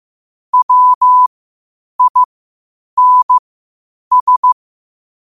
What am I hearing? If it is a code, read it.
Morse code, heard as a keyed tone, about 15 wpm, spaced as for 8 wpm, 1000 Hz. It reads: WINS
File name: qotdfiles / 8wpm